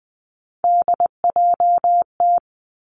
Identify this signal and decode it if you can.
Morse code, heard as a keyed tone, about 20 wpm, 700 Hz. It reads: DJT
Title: Three Challenging Letter-Number Combinations 20wpm